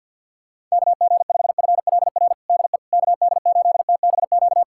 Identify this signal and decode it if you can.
Morse code, heard as a keyed tone, about 46 wpm, 700 Hz. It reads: KG5FLR DE KD8TBQ
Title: USA Callsign DE Callsign 50wpm